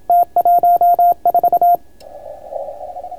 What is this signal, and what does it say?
Morse code, heard as a keyed tone, about 27 wpm, 1400 Hz, with radio noise. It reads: T14